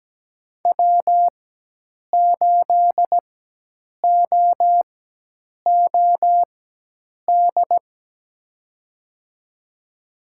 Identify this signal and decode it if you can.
Morse code, heard as a keyed tone, about 17 wpm, spaced as for 8 wpm, 700 Hz. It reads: W8OOD
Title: USA Call Signs - 5 Characters - 4x Character Spacing - 17wpm